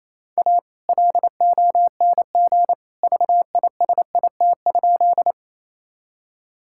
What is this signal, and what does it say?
Morse code, heard as a keyed tone, about 28 wpm, 700 Hz. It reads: A LONG VISIT?
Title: Sentences from Top 1000 Words - Review Part 6 - 28wpm